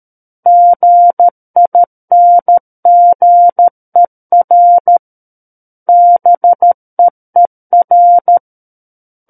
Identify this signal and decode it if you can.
Morse code, heard as a keyed tone, about 13 wpm, 700 Hz. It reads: GINGER BEER
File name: Book One / 04-Chapter 4 - 13 WPM